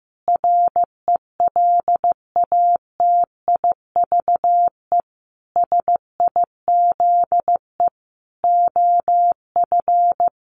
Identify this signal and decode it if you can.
Morse code, heard as a keyed tone, about 15 wpm, 700 Hz. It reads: RELATIVE SIZE OF
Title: Sentences from Top 1400 Words 15wpm